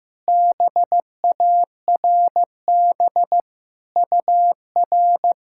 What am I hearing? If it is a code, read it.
Morse code, heard as a keyed tone, about 15 wpm, 700 Hz. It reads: BARB UR
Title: SOTA QSO Practice 15wpm